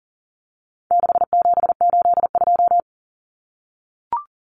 Morse code, heard as a keyed tone, about 40 wpm, 700 Hz. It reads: 6782
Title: Common Cut Numbers - 4-Digit 40wpm